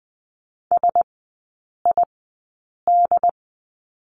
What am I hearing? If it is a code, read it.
Morse code, heard as a keyed tone, about 20 wpm, spaced as for 8 wpm, 700 Hz. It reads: SID
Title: 074 Lesson 13e - Letter L - Three Characters 20wpm